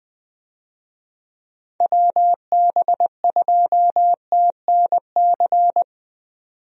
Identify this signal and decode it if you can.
Morse code, heard as a keyed tone, about 20 wpm, 700 Hz. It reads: WB2TNC
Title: USA Call Signs - 6 Characters - 20wpm